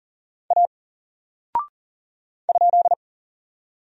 Morse code, heard as a keyed tone, about 40 wpm, 700 Hz. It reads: A ?
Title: Single Character (Letter:Number:Punctuation:Prosign) 40wpm